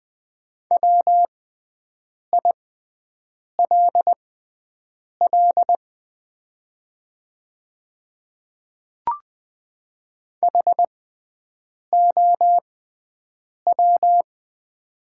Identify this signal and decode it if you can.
Morse code, heard as a keyed tone, about 20 wpm, spaced as for 7 wpm, 700 Hz. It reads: WILL HOW
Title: 6x Character Spacing - Top 100 Words - 20wpm